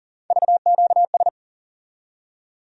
Morse code, heard as a keyed tone, about 40 wpm, 700 Hz. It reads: VQS
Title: USA Call Sign Postfixes - 3 Letters 40wpm